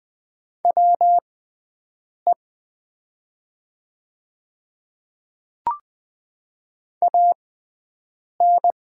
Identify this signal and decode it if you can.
Morse code, heard as a keyed tone, about 20 wpm, spaced as for 7 wpm, 700 Hz. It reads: WE AN
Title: 6x Character Spacing - Top 500 Words - 20wpm